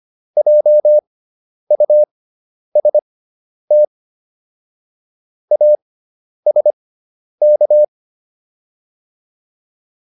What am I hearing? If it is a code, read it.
Morse code, heard as a keyed tone, about 25 wpm, spaced as for 10 wpm, 600 Hz. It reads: JUST ASK